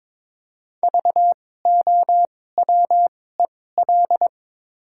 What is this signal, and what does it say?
Morse code, heard as a keyed tone, about 21 wpm, spaced as for 16 wpm, 700 Hz. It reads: VOWEL
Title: 2x Character Spacing - Top 500 Words - 22wpm